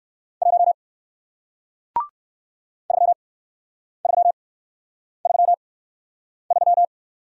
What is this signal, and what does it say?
Morse code, heard as a keyed tone, about 53 wpm, spaced as for 9 wpm, 700 Hz. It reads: 1 3333
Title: Speed-Racing Single Letter-Number 45wpm